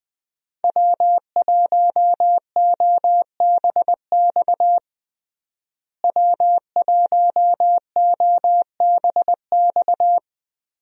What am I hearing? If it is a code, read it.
Morse code, heard as a keyed tone, about 20 wpm, 700 Hz. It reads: W1OBX W1OBX
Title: USA Call Signs - 5 Characters - Repeated 3x - 20wpm